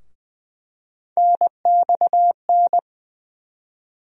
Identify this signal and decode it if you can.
Morse code, heard as a keyed tone, about 20 wpm, 700 Hz. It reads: NXN
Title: USA Call Sign Postfixes - 3 Letters 20wpm